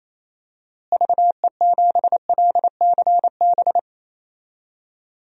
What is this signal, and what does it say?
Morse code, heard as a keyed tone, about 28 wpm, 700 Hz. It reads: VE7LCB
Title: DXCC-100 Call Signs - 6 Characters - Repeated 3x - 28wpm